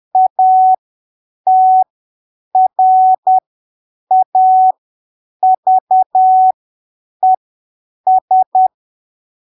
Morse code, heard as a keyed tone, about 10 wpm, spaced as for 7 wpm, 750 Hz. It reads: ATRAVES
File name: arquivos / texto / texto 15 10ppm